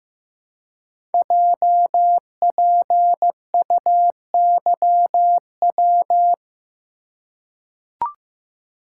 Morse code, heard as a keyed tone, about 15 wpm, 700 Hz. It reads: JPUYW
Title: Code Groups - 5 Letters - 15wpm